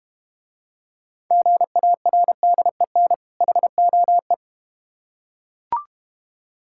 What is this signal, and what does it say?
Morse code, heard as a keyed tone, about 32 wpm, 700 Hz. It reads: GARDEN HOE